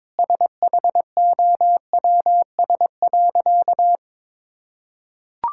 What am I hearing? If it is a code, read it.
Morse code, heard as a keyed tone, about 23 wpm, 700 Hz. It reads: SHOWS.
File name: Difficult Sentences with Hints - 22wpm